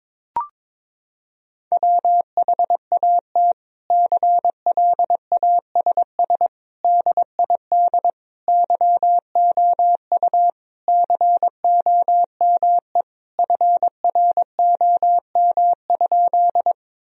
Morse code, heard as a keyed tone, about 22 wpm, 700 Hz. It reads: WHAT CLASS DID YOU COME FROM?
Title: Sentences from Top 1000 Words - Review Part 2 - 22wpm